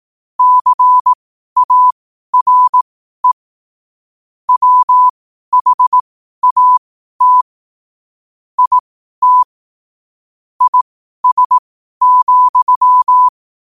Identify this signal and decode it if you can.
Morse code, heard as a keyed tone, about 18 wpm, spaced as for 12 wpm, 1000 Hz. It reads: CARE WHAT IT IS,
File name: qotdfiles / 12wpm